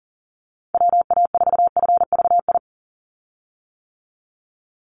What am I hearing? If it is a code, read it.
Morse code, heard as a keyed tone, about 37 wpm, 700 Hz. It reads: WA4FVI
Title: USA Call Signs 40wpm